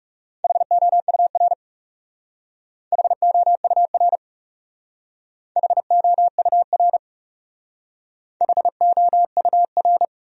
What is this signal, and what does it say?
Morse code, heard as a keyed tone, about 40 wpm, 700 Hz. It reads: HOUR HOUR HOUR HOUR